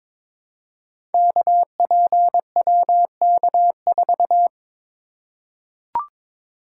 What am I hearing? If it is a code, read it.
Morse code, heard as a keyed tone, about 22 wpm, 700 Hz. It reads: KPWK4
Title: Code Groups - 5 Letter:Numbers - 22wpm